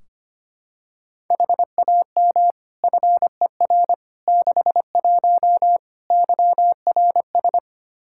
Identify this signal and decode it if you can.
Morse code, heard as a keyed tone, about 25 wpm, 700 Hz. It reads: HAM FER 61 YRS
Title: K1USN SST QSO Practice 25wpm